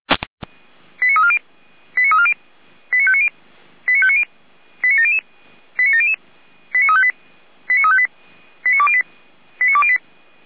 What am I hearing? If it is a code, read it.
Five-tone selective call (ZVEI), heard as quick runs of five tones, tones of about 70 ms. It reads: 89340 89340 89590 89590 89700 89700 89358 89358 89208 89208